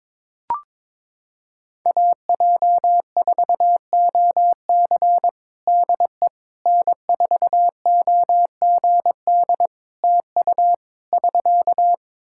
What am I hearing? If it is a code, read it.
Morse code, heard as a keyed tone, about 22 wpm, 700 Hz. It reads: AJ4OC DE N4OGD TU <SK>
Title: Practice QSO - Exchange 4 - Ending - 22wpm